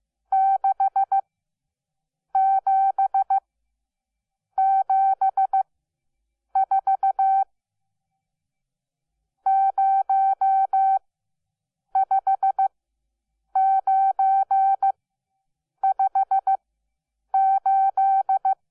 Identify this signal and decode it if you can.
Morse code, heard as a keyed tone, about 15 wpm, spaced as for 7 wpm, 770 Hz. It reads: 67 74 05958
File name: Onlinelehrgang / morsen / DJ4UF Morsen12